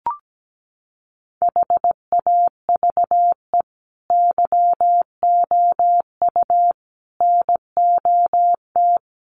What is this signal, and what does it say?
Morse code, heard as a keyed tone, about 17 wpm, 700 Hz. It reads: HAVE YOU NOT
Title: Sentences from Top 1200 Words 17wpm